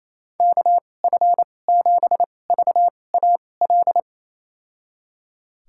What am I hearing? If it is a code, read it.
Morse code, heard as a keyed tone, about 28 wpm, spaced as for 20 wpm, 700 Hz. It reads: KF7VAL
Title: USA Call Signs - 6 Characters - 2x Character Spacing - 28wpm